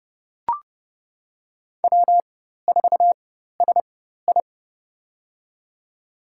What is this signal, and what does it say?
Morse code, heard as a keyed tone, about 30 wpm, spaced as for 14 wpm, 700 Hz. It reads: W4SI
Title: USA Call Signs - 4 Characters - 4x Character Spacing - 30wpm